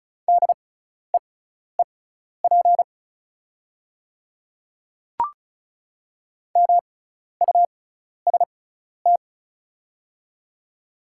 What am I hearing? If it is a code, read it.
Morse code, heard as a keyed tone, about 36 wpm, spaced as for 12 wpm, 700 Hz. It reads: DEEP MUST
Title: 6x Character Spacing - Top 500 Words - 35wpm